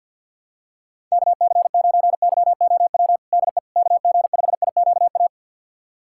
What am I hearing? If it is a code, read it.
Morse code, heard as a keyed tone, about 49 wpm, 700 Hz. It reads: KC9YOW DE KG5IQA
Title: USA Callsign DE Callsign - 6 Characters - 50wpm